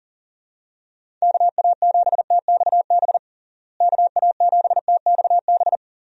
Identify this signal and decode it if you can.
Morse code, heard as a keyed tone, about 40 wpm, 700 Hz. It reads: KA7TXB KA7TXB
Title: USA Call Signs - 6 Characters - Repeated 2x - 40wpm